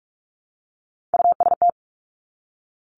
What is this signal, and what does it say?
Morse code, heard as a keyed tone, about 46 wpm, 700 Hz. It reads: UST
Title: Three Challenging Letter-Number Combinations 45wpm